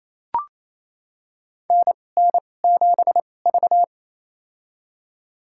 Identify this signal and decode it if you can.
Morse code, heard as a keyed tone, about 28 wpm, spaced as for 20 wpm, 700 Hz. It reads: NN7V
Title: USA Call Signs - 4 Characters - 2x Character Spacing - 28wpm